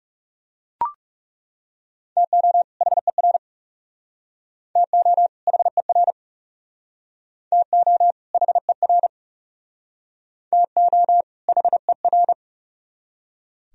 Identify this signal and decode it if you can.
Morse code, heard as a keyed tone, about 40 wpm, 700 Hz. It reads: TO HER TO HER TO HER TO HER